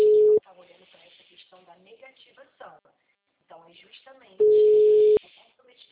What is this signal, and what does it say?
Ringback tone (Europe, 425 Hz), with speech.
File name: out- -2018- - - .82635